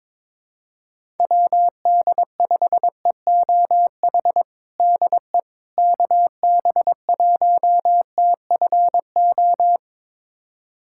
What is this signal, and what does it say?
Morse code, heard as a keyed tone, about 22 wpm, 700 Hz. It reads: WD5EOH DE KB1TFO
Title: USA Callsign DE Callsign - 6 Characters - 22wpm